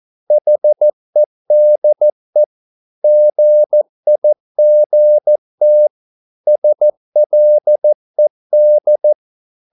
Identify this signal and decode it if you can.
Morse code, heard as a keyed tone, about 14 wpm, 600 Hz. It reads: HEDE GIGT SLED
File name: Korte ord DK 14wpm